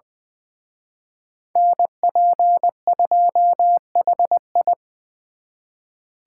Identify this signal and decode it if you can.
Morse code, heard as a keyed tone, about 23 wpm, 700 Hz. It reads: NP2HI
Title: Speed Racing - USA 2x2 Call Signs 20wpm